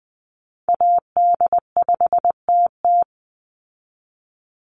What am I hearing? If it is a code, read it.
Morse code, heard as a keyed tone, about 20 wpm, 700 Hz. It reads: AD5TT
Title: Call Signs Repeated 3x - 20wpm